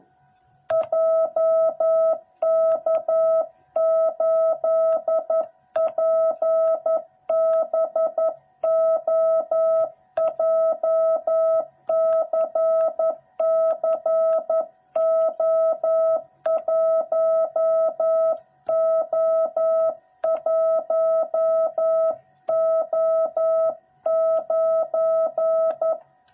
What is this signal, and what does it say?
Morse code, heard as a keyed tone, about 11 wpm, 660 Hz. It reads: JK8PBO JCCO1O1O9
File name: jh8cbh / 2021cw / cw06b